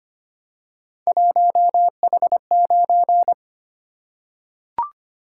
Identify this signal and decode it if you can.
Morse code, heard as a keyed tone, about 25 wpm, 700 Hz. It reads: 1H9
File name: Three Letter-Number Combinations - 25wpm